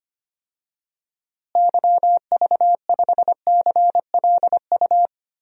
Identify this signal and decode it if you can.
Morse code, heard as a keyed tone, about 25 wpm, 700 Hz. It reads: YV5CLU
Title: DXCC-100 Call Signs - 6 Characters - Repeated 2x - 25wpm